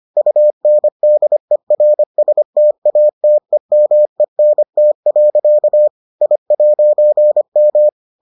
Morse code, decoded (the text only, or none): UNDERSTATEMENT. I'M